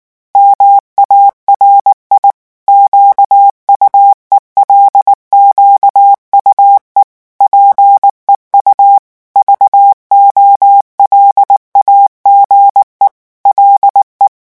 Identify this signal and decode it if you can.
Morse code, heard as a keyed tone, about 19 wpm, 800 Hz. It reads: MARI QUELQUE PEU VOLAGE LE
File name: morse mp3 / revision9 citations humoristiques vitesse 18 mots minute